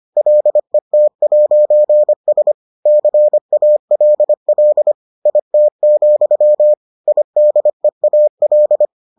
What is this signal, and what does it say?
Morse code, heard as a keyed tone, about 25 wpm, 600 Hz. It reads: LET'S CALL IT, IDEAL